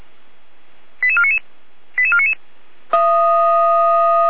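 Five-tone selective call (ZVEI), heard as quick runs of five tones, tones of about 70 ms. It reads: 88499 88499
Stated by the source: Five-tone ZVEI selective call, received on SDR.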